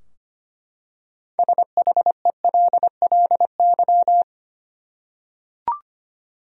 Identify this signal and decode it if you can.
Morse code, heard as a keyed tone, about 25 wpm, 700 Hz. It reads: SHELLY
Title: USA Common Names - 25wpm